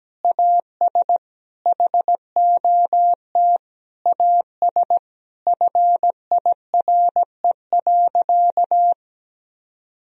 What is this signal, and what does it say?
Morse code, heard as a keyed tone, about 17 wpm, 700 Hz. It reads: AS HOT AS FIRE.
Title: Sentences from Top 1000 Words - Review Part 3 - 17wpm